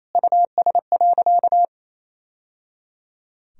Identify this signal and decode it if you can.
Morse code, heard as a keyed tone, about 28 wpm, 700 Hz. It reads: US.
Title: Sentences from Top 1000 Words - Review Part 3 - 28wpm